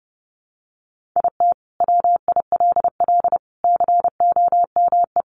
Morse code, heard as a keyed tone, about 30 wpm, 700 Hz. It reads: IT WILL COME
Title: Sentences from Top 1400 Words 30wpm